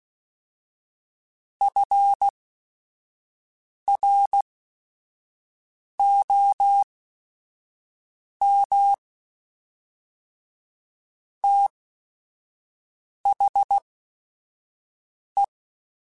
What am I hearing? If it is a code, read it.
Morse code, heard as a keyed tone, about 16 wpm, spaced as for 5 wpm, 800 Hz. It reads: FROM THE
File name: Weather Sets / Weather Sets-mp3s / Weather 4 21.@5